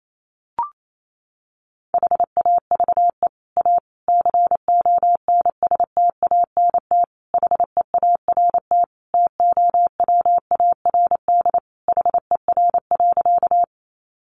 Morse code, heard as a keyed tone, about 28 wpm, 700 Hz. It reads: HAVE A CONSTANT HEART TOWARD HER.
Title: Sentences from Top 1500 Words 28wpm